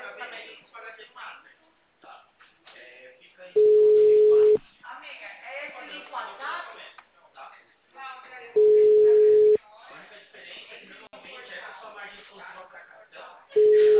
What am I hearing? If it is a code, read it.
Ringback tone (Europe, 425 Hz), with speech.